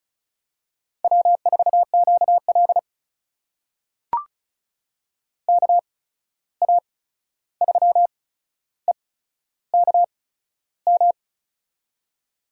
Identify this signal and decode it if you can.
Morse code, heard as a keyed tone, about 36 wpm, spaced as for 13 wpm, 700 Hz. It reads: W4QL KA3EKM
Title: USA Call Signs - 8x Character Spacing - 35wpm